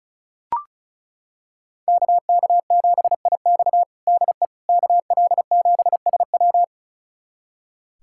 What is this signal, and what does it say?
Morse code, heard as a keyed tone, about 36 wpm, 700 Hz. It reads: KK7IX DE KL7SW
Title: USA Callsign DE Callsign - 5 Characters - 35wpm